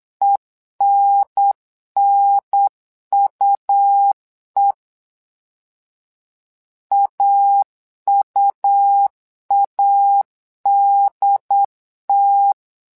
Morse code, heard as a keyed tone, about 8 wpm, 800 Hz. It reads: ENNUE AUADT
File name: LEZIONI / 2 UHDN / 2 UHDN 8wpm2232